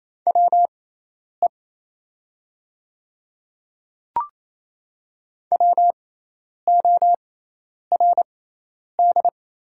Morse code, heard as a keyed tone, about 28 wpm, spaced as for 10 wpm, 700 Hz. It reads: WE WORD
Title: 6x Character Spacing - Top 100 Words - 28wpm